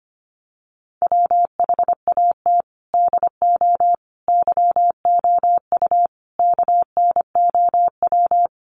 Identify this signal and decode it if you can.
Morse code, heard as a keyed tone, about 25 wpm, 700 Hz. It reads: WHAT DO YOU KNOW
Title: Sets of 5 Words Encore 25wpm